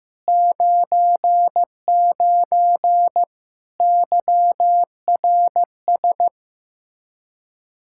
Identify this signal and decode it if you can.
Morse code, heard as a keyed tone, about 15 wpm, 700 Hz. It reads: 99 YRS
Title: Practice QSO - Exchange 3 - More Info - 15wpm